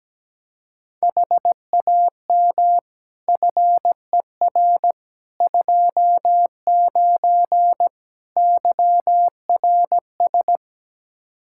Morse code, heard as a keyed tone, about 17 wpm, 700 Hz. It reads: HAM FER 29 YRS
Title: K1USN SST QSO Practice 17wpm